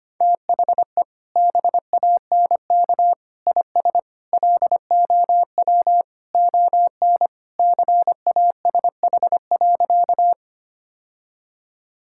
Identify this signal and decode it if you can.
Morse code, heard as a keyed tone, about 25 wpm, 700 Hz. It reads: THE BANK IS LOW ON CASH.
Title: Sentences from Top 1000 Words - Review Part 5 - 25wpm